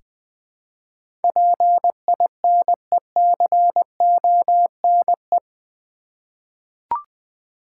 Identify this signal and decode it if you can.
Morse code, heard as a keyed tone, about 20 wpm, 690 Hz. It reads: PINECONE
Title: Difficult Words with Hints - 20wpm